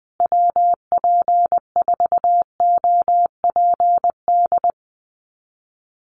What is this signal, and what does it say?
Morse code, heard as a keyed tone, about 20 wpm, 700 Hz. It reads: WP4OPD
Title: USA Call Signs Repeated 2x 20wpm